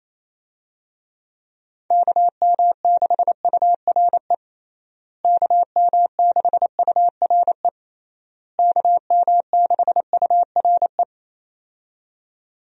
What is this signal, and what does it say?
Morse code, heard as a keyed tone, about 28 wpm, 700 Hz. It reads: KM6URE KM6URE KM6URE
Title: USA Call Signs - 6 Characters - Repeated 3x - 28wpm